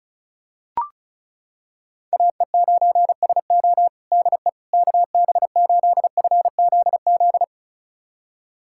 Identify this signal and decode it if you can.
Morse code, heard as a keyed tone, about 34 wpm, 700 Hz. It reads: AE9SO DE KB8FZZ
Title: USA Callsign DE Callsign 35wpm